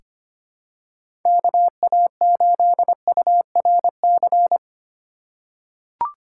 Morse code, heard as a keyed tone, about 25 wpm, 700 Hz. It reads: KA8URC
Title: USA Call Signs - 6 Characters - 3x Character Spacing - 25wpm